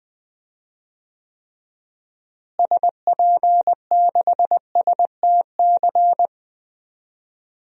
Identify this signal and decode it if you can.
Morse code, heard as a keyed tone, about 21 wpm, 700 Hz. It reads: SP6STC
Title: DXCC-100 Call Signs - 6 Characters - Repeated 3x - 20wpm